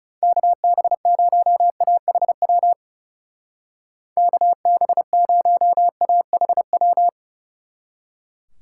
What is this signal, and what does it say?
Morse code, heard as a keyed tone, about 37 wpm, 700 Hz. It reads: KB0AHW KB0AHW
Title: Speed Racing - USA Call Signs 30wpm